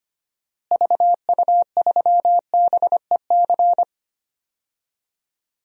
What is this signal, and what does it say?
Morse code, heard as a keyed tone, about 25 wpm, 700 Hz. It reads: VU3BEC
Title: DXCC-100 Call Signs - 6 Characters - Repeated 2x - 25wpm